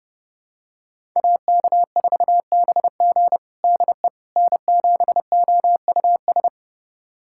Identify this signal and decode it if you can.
Morse code, heard as a keyed tone, about 32 wpm, 700 Hz. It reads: AK4BG DE N7OUS
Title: USA Callsign DE Callsign - 5 Characters - 30wpm